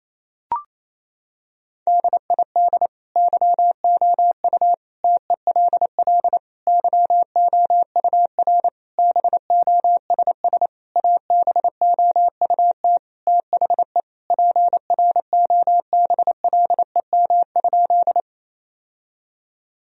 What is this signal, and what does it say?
Morse code, heard as a keyed tone, about 27 wpm, 700 Hz. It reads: DID YOU TELL YOUR BOSS ABOUT THE PROBLEM?
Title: Sentences from Top 1000 Words - Review Part 4 - 28wpm